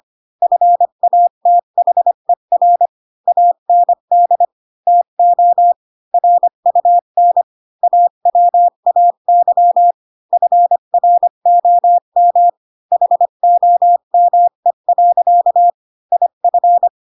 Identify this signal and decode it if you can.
Morse code, heard as a keyed tone, about 25 wpm, 700 Hz. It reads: FATHER AND TO RUN AWAY FROM HOME. IF
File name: Pinocchio / Pinocchio ch5 20 25 ik1hge